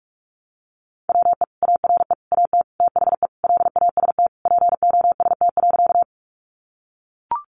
Morse code, heard as a keyed tone, about 44 wpm, 700 Hz. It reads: WE ARE AT THE LAST POST.
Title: Sentences from Top 1000 Words 45wpm